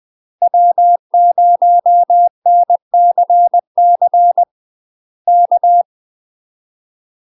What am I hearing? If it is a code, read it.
Morse code, heard as a keyed tone, about 20 wpm, 700 Hz. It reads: W0NCC K
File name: UK MASTER QSOs TRAINING / 20WPM / UK MASTER QSOs 0061